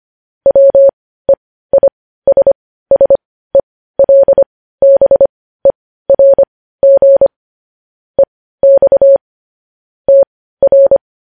Morse code, heard as a keyed tone, about 25 wpm, spaced as for 15 wpm, 550 Hz. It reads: WEISSELBERG EX TR
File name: podcast / News headlines-1x-25x15